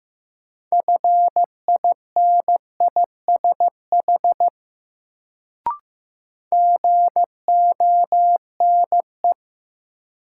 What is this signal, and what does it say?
Morse code, heard as a keyed tone, about 15 wpm, 700 Hz. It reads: FINISH GONE
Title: Sentences from Top 1200 Words - New Word Drill Alt - 15wpm